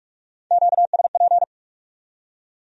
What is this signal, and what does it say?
Morse code, heard as a keyed tone, about 57 wpm, 700 Hz. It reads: QSP